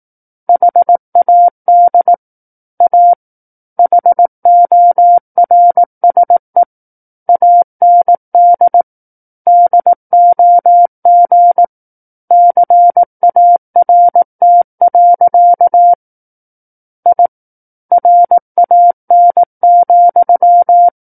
Morse code, heard as a keyed tone, about 18 wpm, 700 Hz. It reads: HAD A HORSE AND DOG CART. I RAN,
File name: 18 WPM